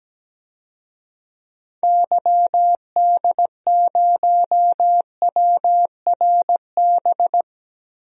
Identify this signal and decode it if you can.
Morse code, heard as a keyed tone, about 17 wpm, 700 Hz. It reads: YD0WRB
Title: DXCC-100 Call Signs - 6 Characters - Repeated 3x - 17wpm